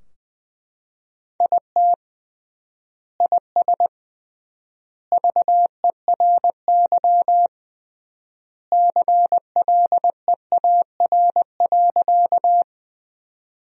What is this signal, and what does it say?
Morse code, heard as a keyed tone, about 20 wpm, 700 Hz. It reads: IT IS VERY CLEAR.
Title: 3x Word-Spacing - Sentences from Top 500 Words - 20wpm